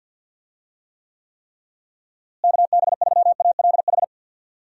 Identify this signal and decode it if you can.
Morse code, heard as a keyed tone, about 50 wpm, 700 Hz. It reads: KB3ALH
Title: USA Call Signs - 6 Characters - Repeated 3x - 50wpm